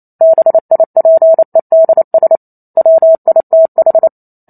Morse code, heard as a keyed tone, about 28 wpm, 650 Hz. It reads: BIPEDS WITH